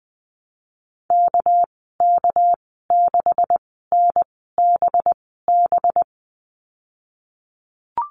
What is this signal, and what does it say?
Morse code, heard as a keyed tone, about 20 wpm, spaced as for 15 wpm, 700 Hz. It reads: KK6NBB
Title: 213 Lesson 34g - Number 8 - Call Signs 20wpm